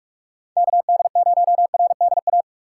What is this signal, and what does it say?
Morse code, heard as a keyed tone, about 46 wpm, 700 Hz. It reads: K D 0 R D A
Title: USA Call Signs - 6 Characters - 6x Character Spacing - 45wpm